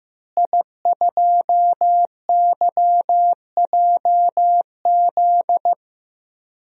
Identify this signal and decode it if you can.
Morse code, heard as a keyed tone, about 15 wpm, 700 Hz. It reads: I2YJZ
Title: DXCC-100 Call Signs - 5 Characters - Repeated 3x - 15wpm